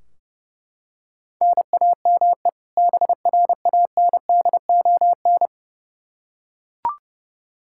Morse code, heard as a keyed tone, about 32 wpm, 700 Hz. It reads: NAME BRANDON
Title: K1USN SST QSO Practice 30wpm